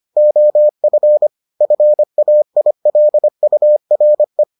Morse code, heard as a keyed tone, about 25 wpm, 600 Hz. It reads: OF FAILURE